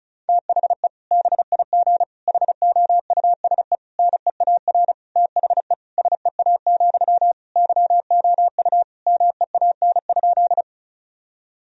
Morse code, heard as a keyed tone, about 36 wpm, 700 Hz. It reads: THE BIG HOUSE NEAR THE SEA, YOU MEAN?
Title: Sentences from Top 1000 Words - Review Part 1 - 35wpm